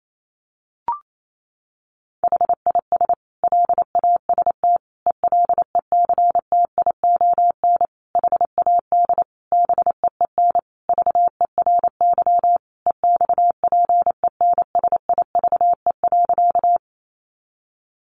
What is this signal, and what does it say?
Morse code, heard as a keyed tone, about 28 wpm, 700 Hz. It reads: HIS LAST ELECTION HAD BEEN VERY EXPENSIVE.
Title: Sentences from Top 1500 Words 28wpm